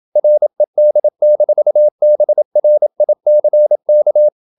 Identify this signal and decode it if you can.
Morse code, heard as a keyed tone, about 27 wpm, 600 Hz. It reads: RED-BRICK
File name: Never 0027wpm